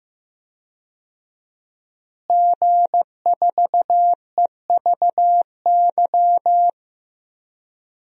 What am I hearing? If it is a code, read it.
Morse code, heard as a keyed tone, about 15 wpm, 700 Hz. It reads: G4EVY
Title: DXCC-100 Call Signs - 5 Characters - Repeated 3x - 15wpm